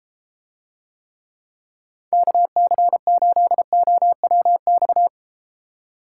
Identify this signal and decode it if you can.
Morse code, heard as a keyed tone, about 34 wpm, 700 Hz. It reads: KC8OWX